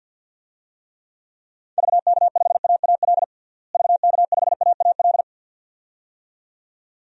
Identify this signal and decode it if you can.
Morse code, heard as a keyed tone, about 47 wpm, 700 Hz. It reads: VK5AAL VK5AAL
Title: DXCC-100 Call Signs - 6 Characters - Repeated 2x - 50wpm